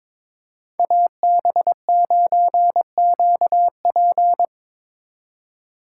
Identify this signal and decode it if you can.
Morse code, heard as a keyed tone, about 21 wpm, 700 Hz. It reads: AB9QP
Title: USA Call Signs - 5 Characters - 3x Character Spacing - 22wpm